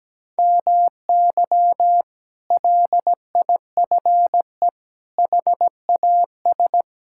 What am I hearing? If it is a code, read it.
Morse code, heard as a keyed tone, about 17 wpm, 700 Hz. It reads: MY LIFE HAS
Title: Sentences from Top 1000 Words - Review Part 6 - 17wpm